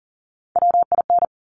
Morse code, heard as a keyed tone, about 40 wpm, 700 Hz. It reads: WIN
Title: Top 1251-1500 Words 40wpm